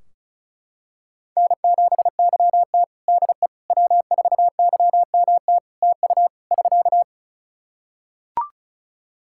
Morse code, heard as a keyed tone, about 36 wpm, 700 Hz. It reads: N7YT DE W4YMT TU <SK>
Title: K1USN SST QSO Practice 35wpm